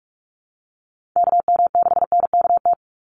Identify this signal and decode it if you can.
Morse code, heard as a keyed tone, about 46 wpm, 700 Hz. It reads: KM6NKT
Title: USA Call Signs 45wpm